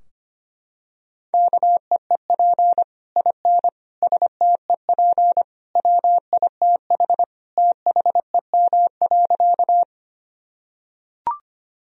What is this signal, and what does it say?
Morse code, heard as a keyed tone, about 25 wpm, 700 Hz. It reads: KEEP IN STEP WITH THEM.
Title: Sentences from Top 1000 Words - Review Part 2 - 25wpm